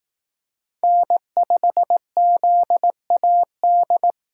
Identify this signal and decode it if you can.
Morse code, heard as a keyed tone, about 18 wpm, 700 Hz. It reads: N5ZAD
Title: Speed Racing - USA 1x3 Call Signs 15wpm